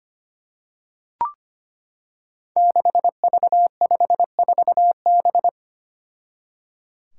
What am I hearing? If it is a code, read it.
Morse code, heard as a keyed tone, about 25 wpm, 700 Hz. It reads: 6V54B
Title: Five Challenging Letter-Number Combinations 25wpm